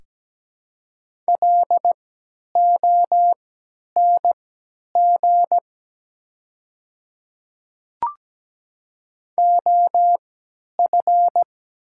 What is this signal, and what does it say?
Morse code, heard as a keyed tone, about 17 wpm, spaced as for 10 wpm, 700 Hz. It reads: LONG OF
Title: 3x Character Spacing - Top 100 Words - 17wpm